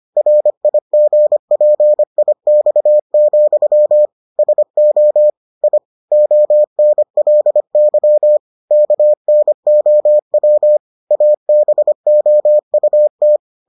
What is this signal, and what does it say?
Morse code, heard as a keyed tone, about 25 wpm, 600 Hz. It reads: RIGPIX, SO I ONLY KNOW ABOUT